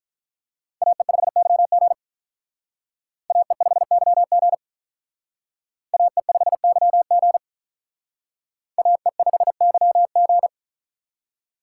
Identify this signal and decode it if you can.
Morse code, heard as a keyed tone, about 45 wpm, 700 Hz. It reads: AE5YG AE5YG AE5YG AE5YG